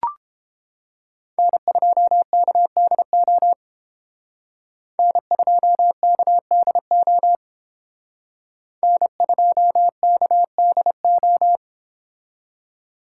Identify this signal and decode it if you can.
Morse code, heard as a keyed tone, about 31 wpm, 700 Hz. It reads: N2KDO N2KDO N2KDO